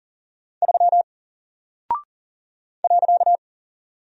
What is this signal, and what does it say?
Morse code, heard as a keyed tone, about 40 wpm, 700 Hz. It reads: 3 .
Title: Single Character (Letter:Number:Punctuation:Prosign) 40wpm